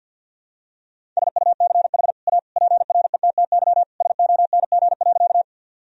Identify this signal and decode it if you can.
Morse code, heard as a keyed tone, about 48 wpm, 700 Hz. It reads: SUCH A PRETTY SONG.
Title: Sentences from Top 1000 Words - Review Part 2 - 50wpm